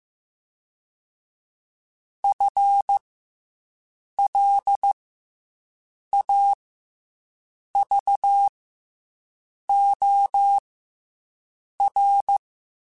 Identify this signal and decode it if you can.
Morse code, heard as a keyed tone, about 15 wpm, 800 Hz. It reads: F L A V O R